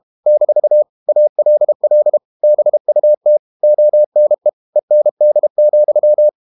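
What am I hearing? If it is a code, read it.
Morse code, heard as a keyed tone, about 32 wpm, 600 Hz. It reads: - ALL BUT ONE END,
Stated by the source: Morse code practice at 32 wpm from CW med Gnister, reading Ken Follett's novel Never in English.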